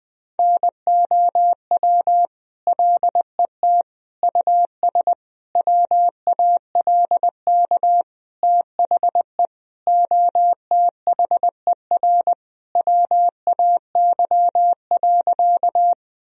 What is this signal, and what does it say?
Morse code, heard as a keyed tone, about 20 wpm, 700 Hz. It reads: NOW LET US WALK THE OTHER WAY.
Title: Sentences from Top 1000 Words - Review Part 1 - 20wpm